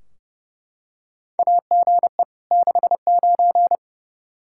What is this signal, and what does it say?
Morse code, heard as a keyed tone, about 30 wpm, 700 Hz. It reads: AGE 69
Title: Practice QSO - Exchange 3 - More Info - 30wpm